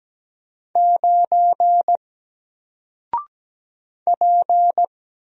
Morse code, heard as a keyed tone, about 17 wpm, 700 Hz. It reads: 9 P
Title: Single Character (Letter:Number:Punctuation:Prosign) 17wpm